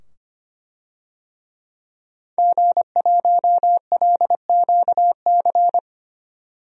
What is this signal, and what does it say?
Morse code, heard as a keyed tone, about 25 wpm, 700 Hz. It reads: G1LQC
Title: DXCC-100 Call Signs - 5 Characters - 25wpm